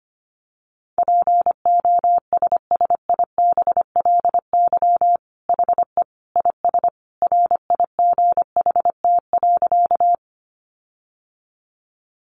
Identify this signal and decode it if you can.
Morse code, heard as a keyed tone, about 25 wpm, 700 Hz. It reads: POSSIBLY HE IS RIGHT.
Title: Sentences from Top 1300 Words 25wpm